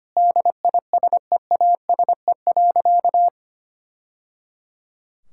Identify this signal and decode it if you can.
Morse code, heard as a keyed tone, about 25 wpm, 700 Hz. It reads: DISEASE.
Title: Sentences from Top 1000 Words - Review Part 6 - 25wpm